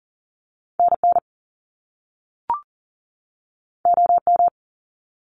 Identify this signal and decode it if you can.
Morse code, heard as a keyed tone, about 39 wpm, 700 Hz. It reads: NN OM
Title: Top Two Letter Combinations in Words 40wpm